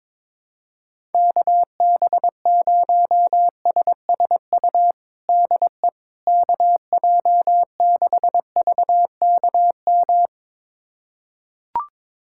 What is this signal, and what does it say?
Morse code, heard as a keyed tone, about 22 wpm, 700 Hz. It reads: KB0SSU DE KJ6VKM
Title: USA Callsign DE Callsign - 6 Characters - 22wpm